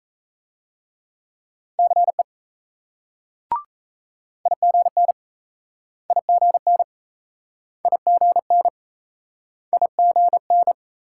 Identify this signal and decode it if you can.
Morse code, heard as a keyed tone, about 38 wpm, 700 Hz. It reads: CE IGN IGN IGN IGN